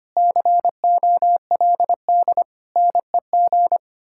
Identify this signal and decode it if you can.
Morse code, heard as a keyed tone, about 33 wpm, 700 Hz. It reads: COLD NEG